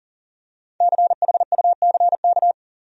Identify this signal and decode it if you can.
Morse code, heard as a keyed tone, about 40 wpm, 700 Hz. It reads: CHUCK
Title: USA Common Names - Max 5 Letters - 40wpm